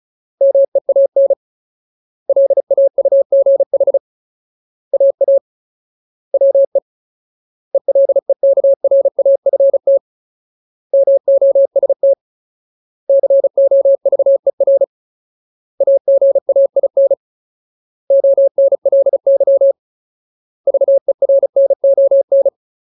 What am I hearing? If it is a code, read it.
Morse code, heard as a keyed tone, about 34 wpm, 550 Hz. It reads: MEAN LAUGH AA WE ELEKRAFT MOST COVER AGAIN ONLY VERNON
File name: Ultimate Word List / 35WPM Ultimate